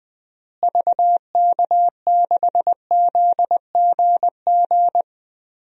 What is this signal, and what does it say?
Morse code, heard as a keyed tone, about 21 wpm, 700 Hz. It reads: VK6ZGG
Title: DXCC-100 Call Signs - 6 Characters - Repeated 2x - 20wpm